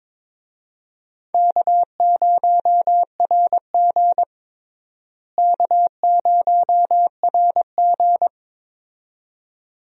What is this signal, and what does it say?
Morse code, heard as a keyed tone, about 22 wpm, 700 Hz. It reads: K0RG K0RG
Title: USA Call Signs - 4 Characters - Repeated 2x - 22wpm